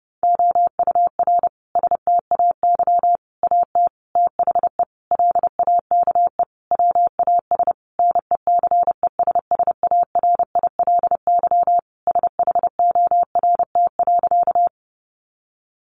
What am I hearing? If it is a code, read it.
Morse code, heard as a keyed tone, about 30 wpm, 700 Hz. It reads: OUR STAY AT THE LAKE WAS NECESSARILY SHORT.
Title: Sentences from Top 1400 Words 30wpm